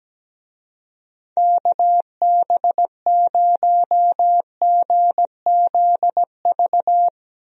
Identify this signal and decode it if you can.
Morse code, heard as a keyed tone, about 17 wpm, 700 Hz. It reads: KB0GZV
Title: USA Call Signs - 6 Characters - Repeated 3x - 17wpm